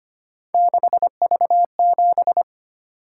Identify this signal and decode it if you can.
Morse code, heard as a keyed tone, about 25 wpm, 700 Hz. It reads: 6V7
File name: Three Letter-Number Combinations - 25wpm